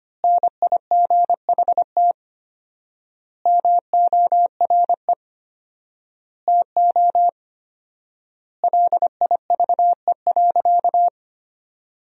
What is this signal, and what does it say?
Morse code, heard as a keyed tone, about 25 wpm, 700 Hz. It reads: NIGHT MORE TO LIVE.
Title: 4x Word-Spacing - Sentences from Top 300 Words - 25wpm